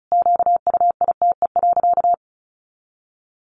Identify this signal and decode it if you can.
Morse code, heard as a keyed tone, about 34 wpm, 700 Hz. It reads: QUITE.
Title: Sentences from Top 1300 Words 35wpm